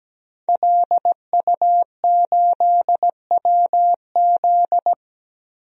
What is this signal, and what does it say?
Morse code, heard as a keyed tone, about 17 wpm, 700 Hz. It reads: LU8WZ
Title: DXCC-100 Call Signs - 5 Characters - Repeated 3x - 17wpm